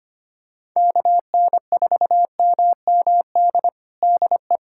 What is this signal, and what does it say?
Morse code, heard as a keyed tone, about 25 wpm, 700 Hz. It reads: KN4MMD DE
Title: USA Callsign DE Callsign - 6 Characters - 25wpm